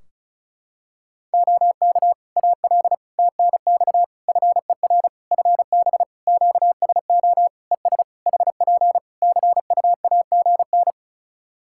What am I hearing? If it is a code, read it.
Morse code, heard as a keyed tone, about 36 wpm, 700 Hz. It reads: OK AL TNX FER FB QSO ES HP CUAGN
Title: Practice QSO - Exchange 4 - Ending - 35wpm